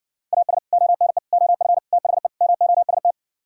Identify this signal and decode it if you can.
Morse code, heard as a keyed tone, about 70 wpm, 700 Hz. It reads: AS ONE OF THE MOST